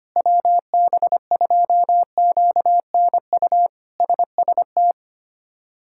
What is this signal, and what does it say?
Morse code, heard as a keyed tone, about 25 wpm, 700 Hz. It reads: WB2QNU SST
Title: K1USN SST QSO Practice - 25wpm